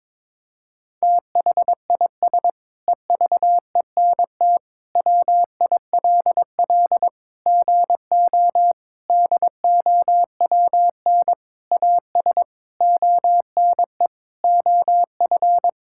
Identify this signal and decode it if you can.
Morse code, heard as a keyed tone, about 22 wpm, 700 Hz. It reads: THIS EVENT WILL GO DOWN AS ONE OF
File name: Sentences with 7-Word Phrases - 22wpm